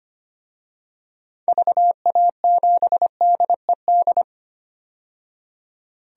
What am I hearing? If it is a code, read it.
Morse code, heard as a keyed tone, about 25 wpm, 700 Hz. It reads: VA7DED
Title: DXCC-100 Call Signs - 6 Characters - 25wpm